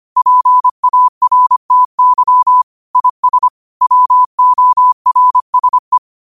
Morse code, heard as a keyed tone, about 25 wpm, 1000 Hz. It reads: PARTY IS WORSE
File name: qotdfiles / 25wpm